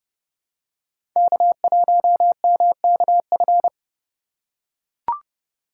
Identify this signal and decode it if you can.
Morse code, heard as a keyed tone, about 32 wpm, 700 Hz. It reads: K1MKF
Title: USA Call Signs - 5 Characters - 4x Character Spacing - 30wpm